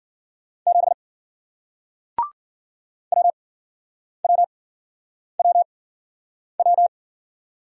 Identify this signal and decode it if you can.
Morse code, heard as a keyed tone, about 49 wpm, spaced as for 9 wpm, 700 Hz. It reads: 7 WWWW
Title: Speed-Racing Single Letter-Number 40wpm